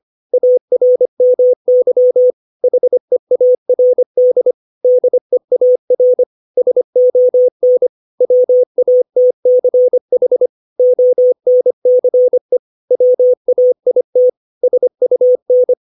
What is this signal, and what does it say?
Morse code, heard as a keyed tone, about 25 wpm, 500 Hz. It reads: ARMY HEARD DEAR SON WATCH ONCE WAIT SUN
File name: 632 words / 632 Words 25WPM